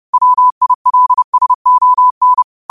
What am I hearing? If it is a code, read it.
Morse code, heard as a keyed tone, about 30 wpm, 1000 Hz. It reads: WILSON